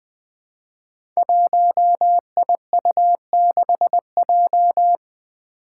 Morse code, heard as a keyed tone, about 20 wpm, 700 Hz. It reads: 1IU6J
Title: Five Challenging Letter-Number Combinations 20wpm